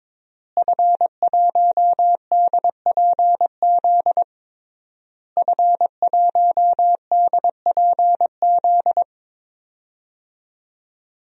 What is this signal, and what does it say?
Morse code, heard as a keyed tone, about 22 wpm, 700 Hz. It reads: F1DPZ F1DPZ
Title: DXCC-100 Call Signs - 5 Characters - Repeated 3x - 22wpm